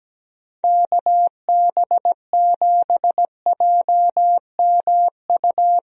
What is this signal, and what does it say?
Morse code, heard as a keyed tone, about 17 wpm, 700 Hz. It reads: KB7JMU
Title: USA Call Signs - 6 Characters - Repeated 3x - 17wpm